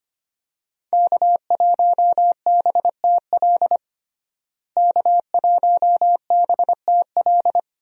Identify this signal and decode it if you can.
Morse code, heard as a keyed tone, about 25 wpm, 700 Hz. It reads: K1BTL K1BTL
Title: USA Call Signs - 5 Characters - Repeated 2x - 25wpm